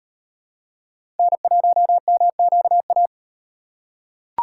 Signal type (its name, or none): Morse code